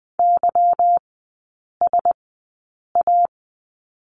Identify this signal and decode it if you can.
Morse code, heard as a keyed tone, about 20 wpm, spaced as for 8 wpm, 700 Hz. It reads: YSA
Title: 166 Lesson 27e - Number 7 - Three Characters 20wpm